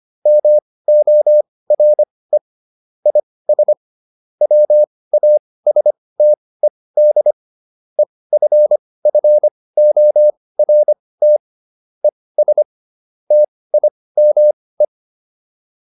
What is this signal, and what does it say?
Morse code, heard as a keyed tone, about 25 wpm, spaced as for 18 wpm, 600 Hz. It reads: MORE IS WASTED EFFORT ES TIME